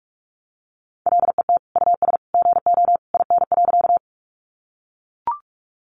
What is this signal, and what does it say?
Morse code, heard as a keyed tone, about 44 wpm, 700 Hz. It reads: LET US GO IN.
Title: Sentences from Top 300 Words 45wpm